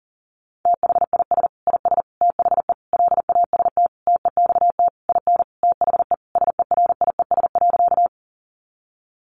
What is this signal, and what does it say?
Morse code, heard as a keyed tone, about 40 wpm, 700 Hz. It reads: THIS IS THE LAST TEXT IN THE SERIES.
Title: Sentences from Top 1200 Words 40wpm